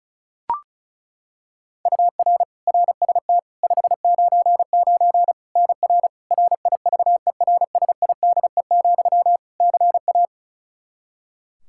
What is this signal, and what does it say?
Morse code, heard as a keyed tone, about 36 wpm, 700 Hz. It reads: UR RST 599 NR RIVERSIDE, CA
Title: Practice QSO - Exchange 1 - Beginning - 35wpm